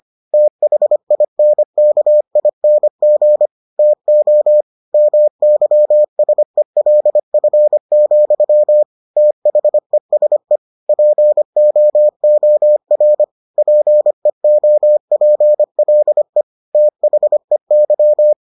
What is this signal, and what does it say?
Morse code, heard as a keyed tone, about 25 wpm, 600 Hz. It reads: THINKING TO MYSELF, THESE POOR PEOPLE THEY